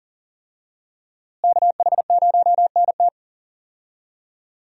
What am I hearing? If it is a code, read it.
Morse code, heard as a keyed tone, about 40 wpm, 700 Hz. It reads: KH0NT
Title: USA Call Signs - 5 Characters - Repeated 2x - 40wpm